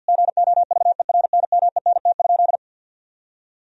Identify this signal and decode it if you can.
Morse code, heard as a keyed tone, about 50 wpm, 700 Hz. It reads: GOVERNMENT?
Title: Sentences from Top 1000 Words - Review Part 4 - 50wpm